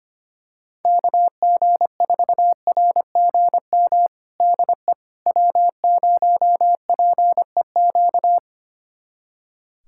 Morse code, heard as a keyed tone, about 25 wpm, 700 Hz. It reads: KG4RGM DE W0PEQ
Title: USA Callsign DE Callsign 25wpm